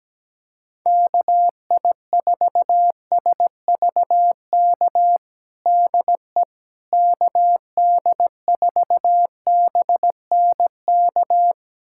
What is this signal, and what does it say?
Morse code, heard as a keyed tone, about 17 wpm, 700 Hz. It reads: KI4SVK DE KD4BNK
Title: USA Callsign DE Callsign - 6 Characters - 17wpm